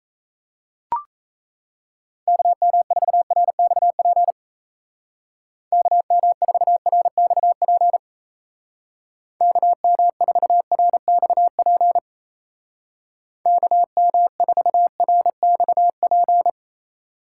Morse code, heard as a keyed tone, about 36 wpm, 700 Hz. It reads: KM4RXP KM4RXP KM4RXP KM4RXP